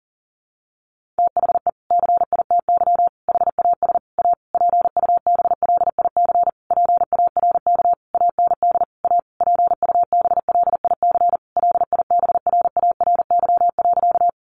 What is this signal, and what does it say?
Morse code, heard as a keyed tone, about 40 wpm, 700 Hz. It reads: THE CITY HAS A PUBLIC PARK AND A PUBLIC LIBRARY.
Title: Sentences from Top 1300 Words 40wpm